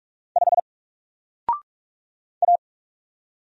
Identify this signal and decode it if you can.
Morse code, heard as a keyed tone, about 43 wpm, 700 Hz. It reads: 5 A
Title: Single Character (Letter:Number:Punctuation:Prosign) Alt 45wpm